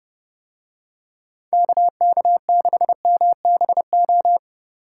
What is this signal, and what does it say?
Morse code, heard as a keyed tone, about 28 wpm, 700 Hz. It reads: KK6MBO